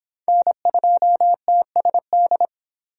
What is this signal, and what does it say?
Morse code, heard as a keyed tone, about 25 wpm, 700 Hz. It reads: N2TSD